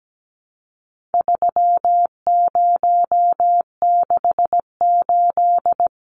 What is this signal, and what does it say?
Morse code, heard as a keyed tone, about 17 wpm, 700 Hz. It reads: 3068
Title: Numbers - 4-Digit 17wpm